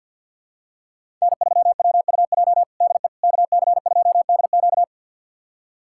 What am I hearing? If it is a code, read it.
Morse code, heard as a keyed tone, about 47 wpm, 700 Hz. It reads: N3WUJ DE KC2DQ
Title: USA Callsign DE Callsign - 5 Characters - 50wpm